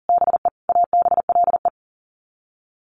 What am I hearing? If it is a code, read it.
Morse code, heard as a keyed tone, about 40 wpm, 700 Hz. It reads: BE ABLE
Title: Sets of 5 Words Encore2 40wpm